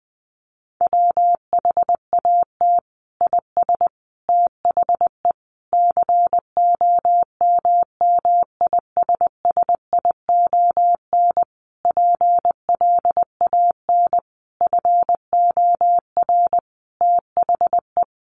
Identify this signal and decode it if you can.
Morse code, heard as a keyed tone, about 20 wpm, 700 Hz. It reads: WHAT IS THE COMMISSION PLAN FOR THE
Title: Sentences from Top 1400 Words 20wpm